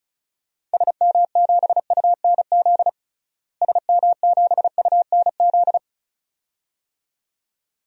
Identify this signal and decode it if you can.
Morse code, heard as a keyed tone, about 34 wpm, 700 Hz. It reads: SM7UNZ SM7UNZ
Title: DXCC-100 Call Signs - 6 Characters - Repeated 2x - 35wpm